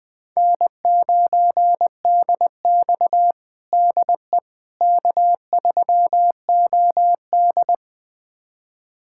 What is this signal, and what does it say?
Morse code, heard as a keyed tone, about 20 wpm, 700 Hz. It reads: N9DX DE K3OD
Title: USA Callsign DE Callsign - 4 Characters - 20wpm